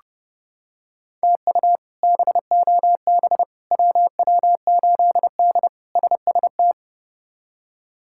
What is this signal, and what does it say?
Morse code, heard as a keyed tone, about 30 wpm, 700 Hz. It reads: TU BOB WW8D SST
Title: K1USN SST QSO Practice - 30wpm